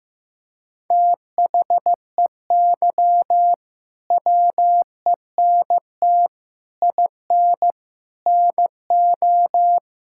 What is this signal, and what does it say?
Morse code, heard as a keyed tone, about 15 wpm, 700 Hz. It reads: THEY WENT IN NO
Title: Sentences from Top 1000 Words - Review Part 5 - 15wpm